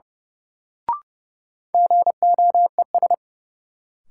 Morse code, heard as a keyed tone, about 30 wpm, 700 Hz. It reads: GOES
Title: Sentences from Top 1000 Words - Review Part 5 - New Word Drill - 30wpm